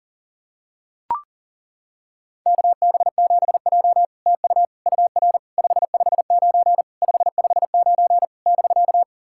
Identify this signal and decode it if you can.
Morse code, heard as a keyed tone, about 40 wpm, 700 Hz. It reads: KB7J TU UR 559 559 <BK>
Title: SOTA QSO Practice 40wpm